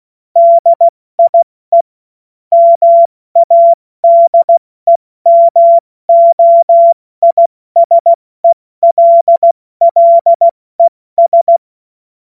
Morse code, heard as a keyed tone, about 16 wpm, 680 Hz. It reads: DIE MADEMOISELLES
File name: Morsekurs / Witze / Witz 19. Oktober 2021, 70 F80